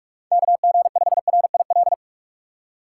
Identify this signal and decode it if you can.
Morse code, heard as a keyed tone, about 47 wpm, 700 Hz. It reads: KG5RIL
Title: USA Call Signs - 6 Characters - 4x Character Spacing - 45wpm